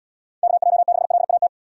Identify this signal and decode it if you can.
Morse code, heard as a keyed tone, about 72 wpm, 700 Hz. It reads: KJ6LAT